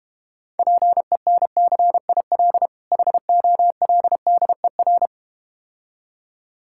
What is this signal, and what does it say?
Morse code, heard as a keyed tone, about 34 wpm, 700 Hz. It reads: PENCIL HOLDER